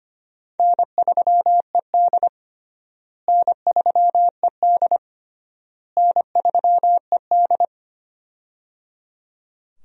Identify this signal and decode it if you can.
Morse code, heard as a keyed tone, about 25 wpm, 700 Hz. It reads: N3ED N3ED N3ED
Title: USA Call Signs - 4 Characters - Repeated 3x - 25wpm